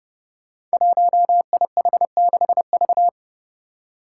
Morse code, heard as a keyed tone, about 30 wpm, 700 Hz. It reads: 1IH6V
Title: Five Challenging Letter-Number Combinations 30wpm